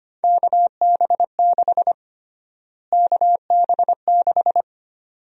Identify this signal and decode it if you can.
Morse code, heard as a keyed tone, about 25 wpm, 700 Hz. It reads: KB6 KB6
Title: USA Call Sign Prefixes - Repeated 2x - 25wpm